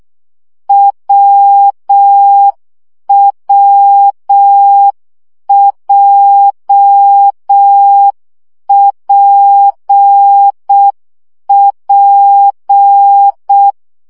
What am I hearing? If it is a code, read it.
Morse code, heard as a keyed tone, about 6 wpm, 800 Hz. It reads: WWJPP
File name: CW gyakorlas w p j 5 baud